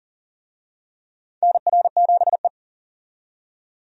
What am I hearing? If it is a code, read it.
Morse code, heard as a keyed tone, about 39 wpm, 700 Hz. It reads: NR7E
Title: USA Call Signs - 4 Characters - 40wpm